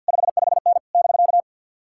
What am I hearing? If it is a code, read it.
Morse code, heard as a keyed tone, about 64 wpm, 700 Hz. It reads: 55N <BK>